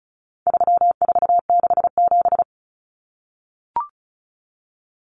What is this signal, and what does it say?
Morse code, heard as a keyed tone, about 34 wpm, 700 Hz. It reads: 3467
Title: Numbers - 4-Digit 35wpm